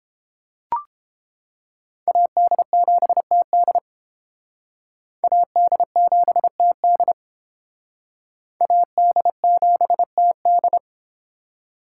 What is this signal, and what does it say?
Morse code, heard as a keyed tone, about 29 wpm, 700 Hz. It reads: AD7TD AD7TD AD7TD